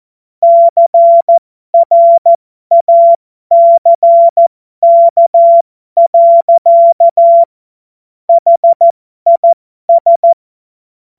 Morse code, heard as a keyed tone, about 14 wpm, spaced as for 12 wpm, 680 Hz. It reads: CRACK. HIS
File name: Morsekurs / Witze